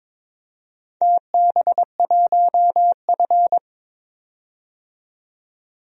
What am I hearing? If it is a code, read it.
Morse code, heard as a keyed tone, about 22 wpm, 700 Hz. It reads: TB1F
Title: DXCC-100 Call Signs - 4 Characters - 22wpm